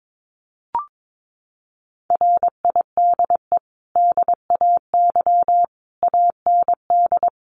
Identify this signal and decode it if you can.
Morse code, heard as a keyed tone, about 22 wpm, 700 Hz. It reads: RIDE DAY AND
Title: Sentences from Top 1200 Words 22wpm